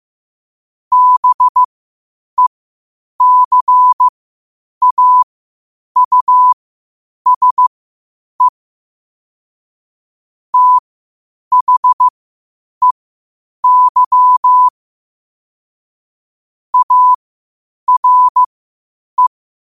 Morse code, heard as a keyed tone, about 15 wpm, spaced as for 8 wpm, 1000 Hz. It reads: BECAUSE THEY ARE
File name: qotdfiles / 8wpm